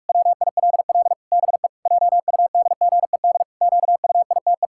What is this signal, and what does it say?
Morse code, heard as a keyed tone, about 46 wpm, 700 Hz. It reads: WILL BE JUDGED QUITE
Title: Pangram Challenge v2 - 45wpm